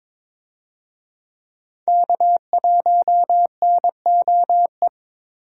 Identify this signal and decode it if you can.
Morse code, heard as a keyed tone, about 21 wpm, 700 Hz. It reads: K1NOE
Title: USA Call Signs - 5 Characters - 22wpm